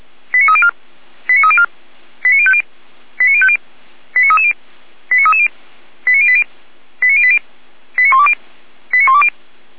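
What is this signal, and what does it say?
Five-tone selective call (ZVEI), heard as quick runs of five tones, tones of about 70 ms. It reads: 89304 89304 89050 89050 89330 89330 89080 89080 89120 89120
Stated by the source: Received on an SDR; ZVEI five-tone call 89125.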